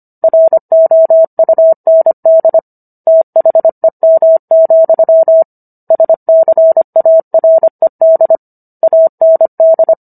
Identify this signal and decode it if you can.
Morse code, heard as a keyed tone, about 25 wpm, 650 Hz. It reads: ROUND THEM, SCARED AND